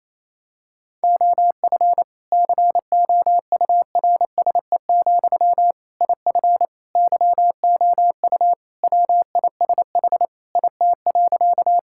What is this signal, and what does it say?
Morse code, heard as a keyed tone, about 28 wpm, 700 Hz. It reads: OF COURSE, IF YOU WISH IT.
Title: Sentences from Top 1600 Words 28wpm